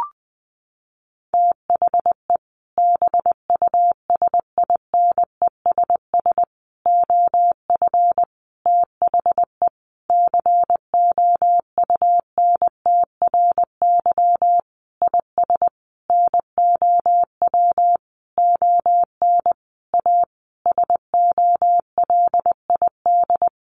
Morse code, heard as a keyed tone, about 20 wpm, 700 Hz. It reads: THE BUSINESS OF THE COUNTRY IS NOW ON A SOLID
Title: Sentences from Top 1500 Words 20wpm